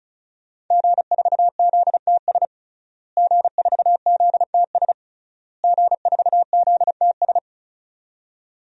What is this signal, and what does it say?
Morse code, heard as a keyed tone, about 36 wpm, 700 Hz. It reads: G4ZTS G4ZTS G4ZTS
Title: DXCC-100 Call Signs - 5 Characters - Repeated 3x - 35wpm